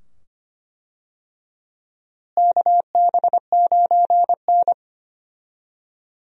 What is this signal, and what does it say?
Morse code, heard as a keyed tone, about 25 wpm, 700 Hz. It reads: KB9N
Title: USA Call Signs - 4 Characters - Repeated 2x - 25wpm